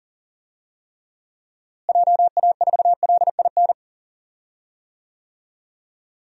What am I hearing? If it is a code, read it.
Morse code, heard as a keyed tone, about 37 wpm, 700 Hz. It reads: JA4LIN
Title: DXCC-100 Call Signs - 6 Characters - Repeated 2x - 40wpm